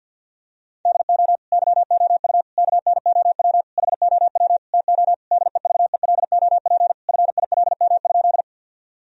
Morse code, heard as a keyed tone, about 50 wpm, 700 Hz. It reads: DO YOU KNOW HOW TO DEVELOP FILM?
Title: Sentences from Top 1000 Words - Review Part 3 - 50wpm